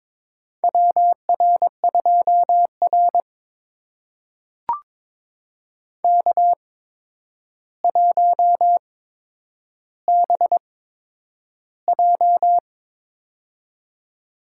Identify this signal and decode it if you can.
Morse code, heard as a keyed tone, about 22 wpm, spaced as for 10 wpm, 700 Hz. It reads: WR2R K1BJ
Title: USA Call Signs - 4 Characters - 8x Character Spacing - 22wpm